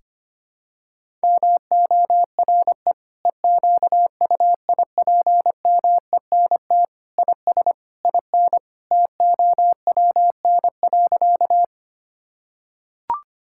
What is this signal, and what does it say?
Morse code, heard as a keyed tone, about 25 wpm, 700 Hz. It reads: MORE EQUIPMENT IS IN TOWN.
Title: Sentences from Top 1000 Words - Review Part 5 - 25wpm